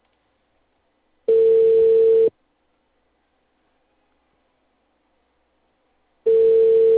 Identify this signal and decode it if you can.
Ringback tone (China, 450 Hz).